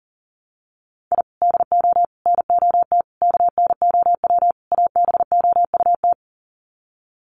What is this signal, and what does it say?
Morse code, heard as a keyed tone, about 40 wpm, 700 Hz. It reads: I DO NOT KNOW ABOUT
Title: Sets of 5 Words 40wpm